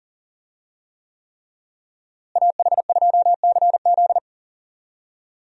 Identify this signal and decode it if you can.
Morse code, heard as a keyed tone, about 40 wpm, 700 Hz. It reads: AH2CZ
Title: USA Call Signs - 5 Characters - Repeated 3x - 40wpm